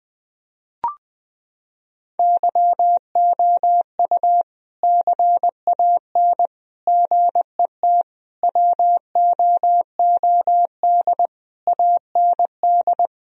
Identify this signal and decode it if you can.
Morse code, heard as a keyed tone, about 20 wpm, 700 Hz. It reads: YOU CAN GET WOOD AND
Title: Sentences from Top 1000 Words - Review Part 2 - 20wpm